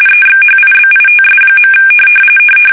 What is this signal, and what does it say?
RTTY, about 45.45 baud. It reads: EXHIBITS.
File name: fsk850 / USS-Massachusetts